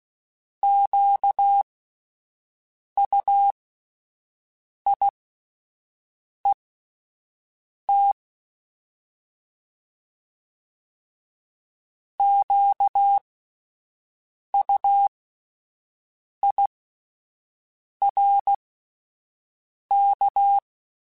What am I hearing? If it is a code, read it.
Morse code, heard as a keyed tone, about 16 wpm, spaced as for 5 wpm, 800 Hz. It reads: QUIET QUIRK
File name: Alphabetical / Alphabetical mp3s / Qs @5